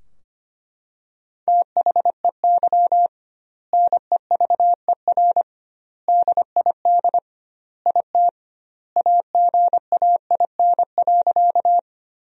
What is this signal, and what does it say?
Morse code, heard as a keyed tone, about 25 wpm, 700 Hz. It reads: THEY NEVER DID IT AGAIN.
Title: 2x Word-Spacing - Sentences from Top 300 Words - 25wpm